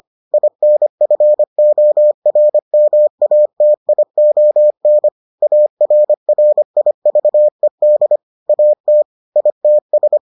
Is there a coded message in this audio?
Yes: Morse code from 0.3 to 10.2 s, heard as a keyed tone, about 25 wpm, 600 Hz. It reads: INFORMATION ARRIVED AT ITS